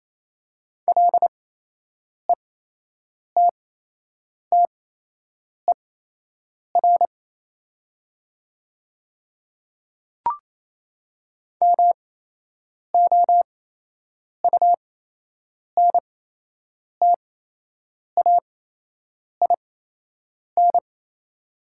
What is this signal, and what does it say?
Morse code, heard as a keyed tone, about 28 wpm, spaced as for 8 wpm, 700 Hz. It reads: LETTER MOUNTAIN
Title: 8x Character Spacing - Top 300 Words - 28wpm